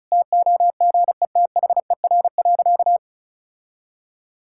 Morse code, heard as a keyed tone, about 36 wpm, 700 Hz. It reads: TOGETHER.
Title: Sentences from Top 1600 Words 35wpm